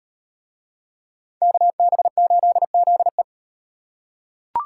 Morse code, heard as a keyed tone, about 34 wpm, 700 Hz. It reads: KB8ZE